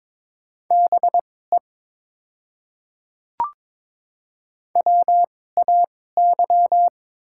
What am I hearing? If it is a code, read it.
Morse code, heard as a keyed tone, about 22 wpm, spaced as for 16 wpm, 690 Hz. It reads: BE WAY
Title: 2x Character Spacing - Top 100 Words - 22wpm